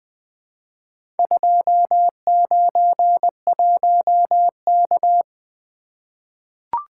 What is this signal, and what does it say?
Morse code, heard as a keyed tone, about 21 wpm, 700 Hz. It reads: 291K
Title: Four Letter-Number Combinations - 20wpm